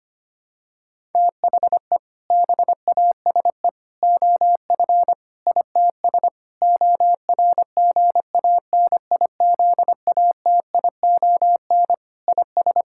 Morse code, heard as a keyed tone, about 25 wpm, 700 Hz. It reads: THE BASE OF ITS ORGANIZATION IS
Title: Sentences from Top 1000 Words - Review Part 5 - 25wpm